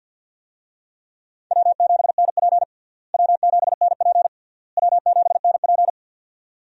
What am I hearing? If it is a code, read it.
Morse code, heard as a keyed tone, about 50 wpm, 700 Hz. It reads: W7NP W7NP W7NP
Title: USA Call Signs - 4 Characters - Repeated 3x - 50wpm